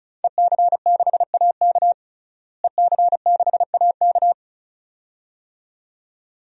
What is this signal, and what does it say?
Morse code, heard as a keyed tone, about 34 wpm, 700 Hz. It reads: EC6AK EC6AK
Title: DXCC-100 Call Signs - 5 Characters - Repeated 2x - 35wpm